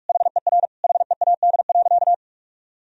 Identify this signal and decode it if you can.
Morse code, heard as a keyed tone, about 46 wpm, 700 Hz. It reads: HER HEAD.
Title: Sentences from Top 300 Words and Abbreviations 45wpm